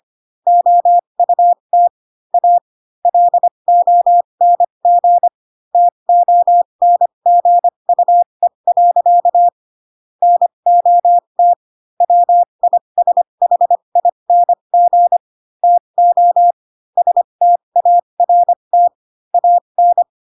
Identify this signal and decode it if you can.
Morse code, heard as a keyed tone, about 25 wpm, spaced as for 22 wpm, 700 Hz. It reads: OUT A LONG TONGUE. NOT WISHING TO START AN
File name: Pinocchio / Pinocchio ch3 18 25 ik1hge